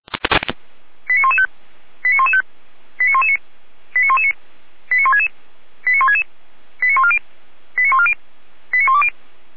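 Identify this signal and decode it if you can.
Five-tone selective call (ZVEI), heard as quick runs of five tones, tones of about 70 ms. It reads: 89115 89115 89119 89119 89166 89166 89140 89140 89120
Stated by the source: ZVEI five-tone selective call received on an SDR.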